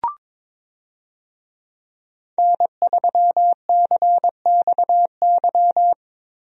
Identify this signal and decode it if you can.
Morse code, heard as a keyed tone, about 22 wpm, 700 Hz. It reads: N3CXY
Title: USA Call Signs - 5 Characters - Repeated 3x - 22wpm